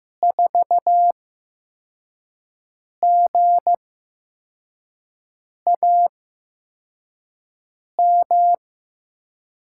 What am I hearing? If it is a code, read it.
Morse code, heard as a keyed tone, about 15 wpm, spaced as for 4 wpm, 700 Hz. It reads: 4GAM
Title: USA Call Signs - 6 Characters - 8x Character Spacing - 15wpm